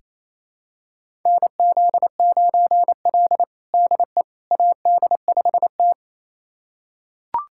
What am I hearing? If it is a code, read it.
Morse code, heard as a keyed tone, about 28 wpm, 700 Hz. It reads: NZ9L DE AD5T
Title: USA Callsign DE Callsign - 4 Characters - 28wpm